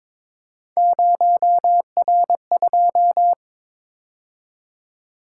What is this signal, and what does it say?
Morse code, heard as a keyed tone, about 23 wpm, 700 Hz. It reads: 0R2
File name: practice / 22wpm / Three Letter-Number Combinations - 22wpm